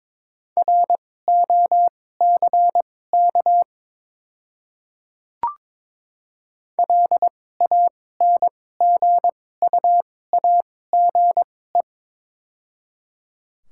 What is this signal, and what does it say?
Morse code, heard as a keyed tone, about 22 wpm, spaced as for 16 wpm, 700 Hz. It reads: ROCK LANGUAGE
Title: 2x Character Spacing - Top 500 Words - 22wpm